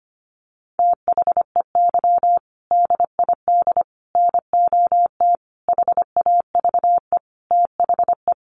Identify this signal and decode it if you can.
Morse code, heard as a keyed tone, about 25 wpm, 700 Hz. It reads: THEY DID NOT HAVE THE
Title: Sentences from Top 1400 Words 25wpm